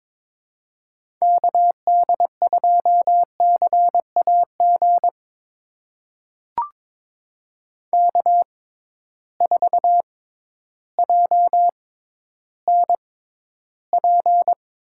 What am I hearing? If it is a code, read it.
Morse code, heard as a keyed tone, about 22 wpm, spaced as for 12 wpm, 700 Hz. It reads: KD2CAG K4JNP
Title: USA Call Signs - 6x Character Spacing - 22wpm